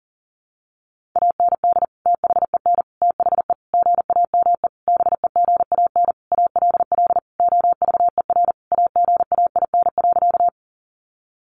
Morse code, heard as a keyed tone, about 40 wpm, 700 Hz. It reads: AND THEN THE GAME BEGAN ALL OVER AGAIN.
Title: Sentences from Top 500 Words 40wpm